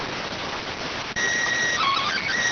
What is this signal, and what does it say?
SSTV (slow-scan television), PD 290.